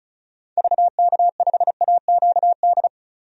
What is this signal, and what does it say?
Morse code, heard as a keyed tone, about 35 wpm, 700 Hz. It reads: VK5AQD
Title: DXCC-100 Call Signs - 6 Characters - 35wpm